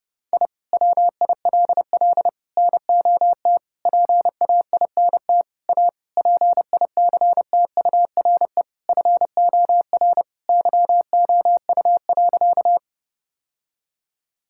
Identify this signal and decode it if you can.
Morse code, heard as a keyed tone, about 30 wpm, 700 Hz. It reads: I WILL NOT PAINT A PICTURE FOR YOU.
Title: Sentences from Top 1000 Words - Review Part 6 - 30wpm